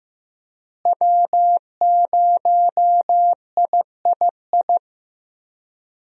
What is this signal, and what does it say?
Morse code, heard as a keyed tone, about 15 wpm, 700 Hz. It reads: W0III
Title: USA Call Signs - 5 Characters - Repeated 3x - 15wpm